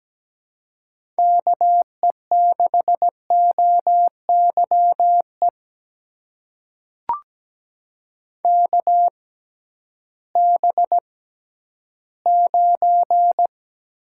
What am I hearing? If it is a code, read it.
Morse code, heard as a keyed tone, about 17 wpm, spaced as for 11 wpm, 700 Hz. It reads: KE6OYE KB9
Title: USA Call Signs - 6 Characters - 6x Character Spacing - 17wpm